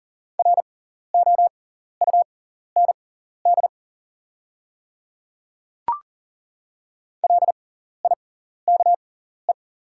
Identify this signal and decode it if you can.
Morse code, heard as a keyed tone, about 39 wpm, spaced as for 14 wpm, 700 Hz. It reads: ROUND LIKE
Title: 6x Character Spacing - Top 300 Words - 40wpm